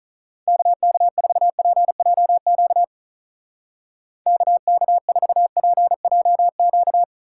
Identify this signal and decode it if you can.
Morse code, heard as a keyed tone, about 43 wpm, 700 Hz. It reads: KK4PJQ KK4PJQ
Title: Speed Racing - USA 2x3 Call Signs 35wpm